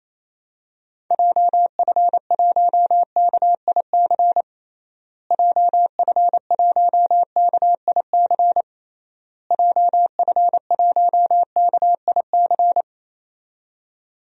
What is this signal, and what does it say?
Morse code, heard as a keyed tone, about 27 wpm, 700 Hz. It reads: JF1KIC JF1KIC JF1KIC
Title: DXCC-100 Call Signs - 6 Characters - Repeated 3x - 28wpm